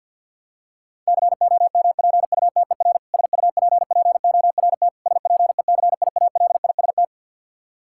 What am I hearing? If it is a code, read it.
Morse code, heard as a keyed tone, about 50 wpm, 700 Hz. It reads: COMPUTER SUPPORT SPECIALIST
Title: Practice QSO - Exchange 3 - More Info - 50wpm